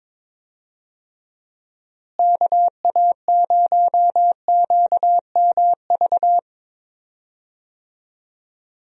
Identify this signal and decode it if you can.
Morse code, heard as a keyed tone, about 22 wpm, 700 Hz. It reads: KA0QMV
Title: USA Call Signs - 6 Characters - Repeated 3x - 22wpm